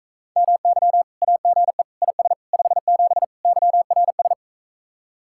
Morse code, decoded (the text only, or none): MY AGE IS 57 YRS